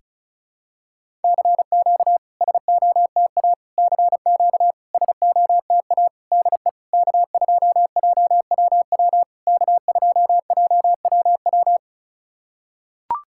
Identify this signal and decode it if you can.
Morse code, heard as a keyed tone, about 34 wpm, 690 Hz. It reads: CQ SOTA CQ SOTA DE K2JWW K2JWW
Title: SOTA QSO Practice 35wpm